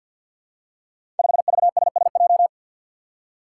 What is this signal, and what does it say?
Morse code, heard as a keyed tone, about 52 wpm, 700 Hz. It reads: 5VSSJ
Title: Five Challenging Letter-Number Combinations 50wpm